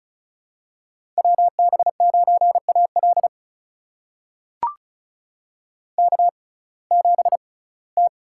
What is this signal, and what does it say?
Morse code, heard as a keyed tone, about 36 wpm, spaced as for 21 wpm, 700 Hz. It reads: WB9AL K7T
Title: USA Call Signs - 5 Characters - 6x Character Spacing - 35wpm